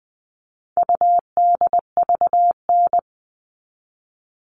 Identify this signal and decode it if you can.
Morse code, heard as a keyed tone, about 20 wpm, 700 Hz. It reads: UDVN
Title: All Cut Numbers - 4-Digit 20wpm